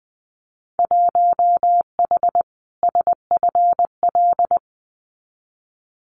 Morse code, heard as a keyed tone, about 20 wpm, 700 Hz. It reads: 1H SFL
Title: Winter Field Day Exchanges 20wpm